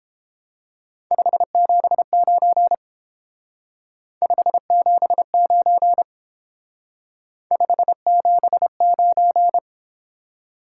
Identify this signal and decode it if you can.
Morse code, heard as a keyed tone, about 28 wpm, 700 Hz. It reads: 579 579 579